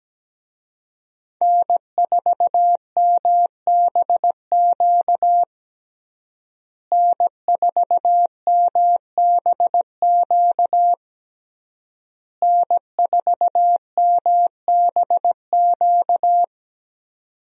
Morse code, heard as a keyed tone, about 17 wpm, 700 Hz. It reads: N4MBQ N4MBQ N4MBQ
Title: USA Call Signs - 5 Characters - Repeated 3x - 17wpm